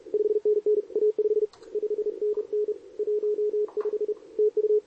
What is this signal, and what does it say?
Morse code, heard as a keyed tone, about 33 wpm, 440 Hz, with radio noise. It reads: 5NN A5 5NN 15 TU
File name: CQWWCW2023 / SP